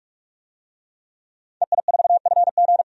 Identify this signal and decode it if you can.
Morse code, heard as a keyed tone, about 46 wpm, 700 Hz. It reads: EI4FG
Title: DXCC-100 Call Signs - 5 Characters - 45wpm